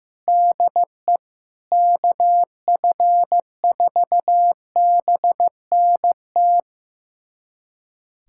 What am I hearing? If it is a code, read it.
Morse code, heard as a keyed tone, about 15 wpm, 700 Hz. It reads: DE KF4BNT
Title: USA Callsign DE Callsign - 6 Characters - 15wpm